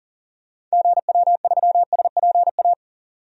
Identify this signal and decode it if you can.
Morse code, heard as a keyed tone, about 40 wpm, 700 Hz. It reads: GW3SPA
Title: DXCC-100 Call Signs - 6 Characters - 40wpm